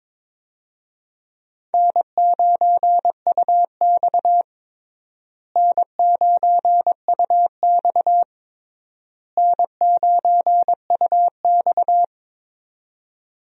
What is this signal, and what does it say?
Morse code, heard as a keyed tone, about 22 wpm, 700 Hz. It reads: N9UX N9UX N9UX
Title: USA Call Signs - 4 Characters - Repeated 3x - 22wpm